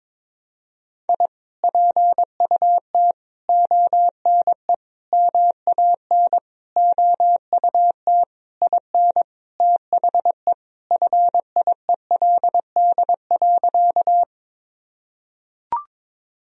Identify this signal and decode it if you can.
Morse code, heard as a keyed tone, about 22 wpm, 700 Hz. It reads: I PUT ONE MAN OUT IN THE FIELD.
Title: Sentences from Top 1000 Words - Review Part 2 - 22wpm